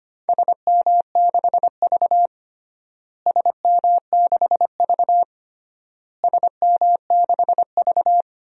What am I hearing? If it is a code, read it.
Morse code, heard as a keyed tone, about 25 wpm, 700 Hz. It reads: SM6V SM6V SM6V
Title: DXCC-100 Call Signs - 4 Characters - Repeated 3x - 25wpm